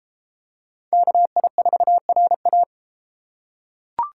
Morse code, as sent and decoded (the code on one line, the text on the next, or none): -.- .. ....- .-. .-
KI4RA